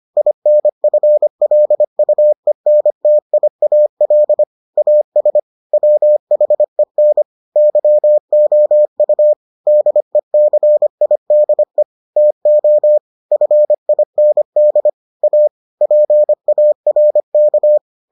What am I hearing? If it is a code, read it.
Morse code, heard as a keyed tone, about 25 wpm, 600 Hz. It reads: INFLUENTIAL AS WHEN YOU DECIDE TO FIND A PARK